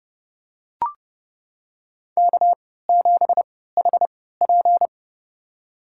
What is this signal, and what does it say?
Morse code, heard as a keyed tone, about 32 wpm, spaced as for 17 wpm, 700 Hz. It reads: K7HP
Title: USA Call Signs - 4 Characters - 3x Character Spacing - 30wpm